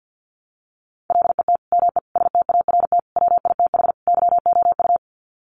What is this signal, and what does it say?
Morse code, heard as a keyed tone, about 50 wpm, 700 Hz. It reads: LET ME START WITH YOU
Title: Sets of 5 Words Encore 50wpm